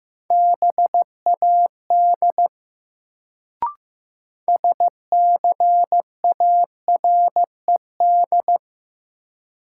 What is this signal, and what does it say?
Morse code, heard as a keyed tone, about 15 wpm, 700 Hz. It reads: BAD SCARED
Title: Sentences from Top 700 Words - New Word Drill - 15wpm